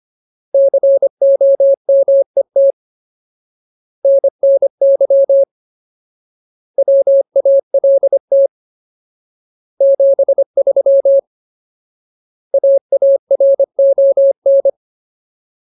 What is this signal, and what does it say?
Morse code, heard as a keyed tone, about 25 wpm, 550 Hz. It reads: COMET NNY WALT 73 AARON
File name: Ultimate Word List / 25WPM Ultimate